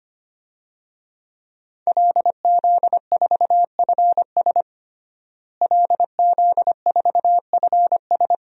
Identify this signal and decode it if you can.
Morse code, heard as a keyed tone, about 25 wpm, 700 Hz. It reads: LZ4FS LZ4FS
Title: DXCC-100 Call Signs - 5 Characters - Repeated 3x - 25wpm